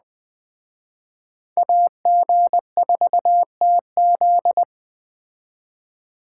Morse code, heard as a keyed tone, about 21 wpm, 700 Hz. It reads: AG4TZ